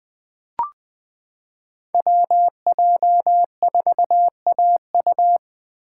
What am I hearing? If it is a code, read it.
Morse code, heard as a keyed tone, about 20 wpm, 700 Hz. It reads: WJ4AU
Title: Code Groups - 5 Letter:Numbers - 20wpm